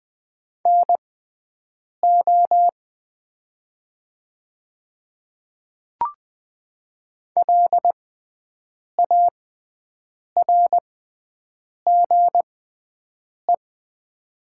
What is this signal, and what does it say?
Morse code, heard as a keyed tone, about 20 wpm, spaced as for 7 wpm, 700 Hz. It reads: NO LARGE
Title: 6x Character Spacing - Top 200 Words - 20wpm